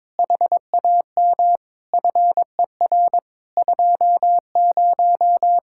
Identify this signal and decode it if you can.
Morse code, heard as a keyed tone, about 21 wpm, 700 Hz. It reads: HAM FER 20
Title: Practice QSO - Exchange 3 - More Info - 22wpm